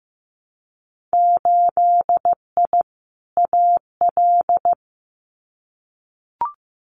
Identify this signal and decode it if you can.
Morse code, heard as a keyed tone, about 15 wpm, 700 Hz. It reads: 8I AL
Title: Winter Field Day Exchanges Encore 15wpm